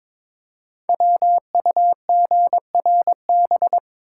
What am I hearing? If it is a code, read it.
Morse code, heard as a keyed tone, about 22 wpm, 700 Hz. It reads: WUGRB
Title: Code Groups - 5 Letters - 22wpm